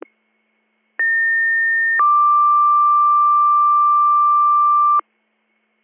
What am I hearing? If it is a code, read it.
Two-tone page, heard as a one-second tone, then a three-second tone, about 1800 then 1180 Hz. It is heard from 1.0 to 5.0 s.